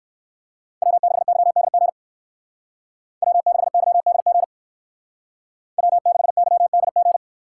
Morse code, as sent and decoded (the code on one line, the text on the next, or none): .-- -.... -.-- -.. --. / .-- -.... -.-- -.. --. / .-- -.... -.-- -.. --.
W6YDG W6YDG W6YDG